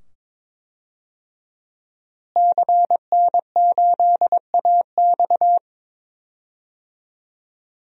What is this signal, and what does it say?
Morse code, heard as a keyed tone, about 22 wpm, 700 Hz. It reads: CN8AX
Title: DXCC-100 Call Signs - 5 Characters - Repeated 2x - 22wpm